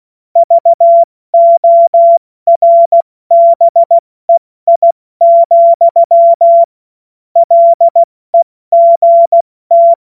Morse code, heard as a keyed tone, about 16 wpm, spaced as for 14 wpm, 680 Hz. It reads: VORBEI, LEGT
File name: Morsekurs / Witze / Witz 15. November 2022 70 F80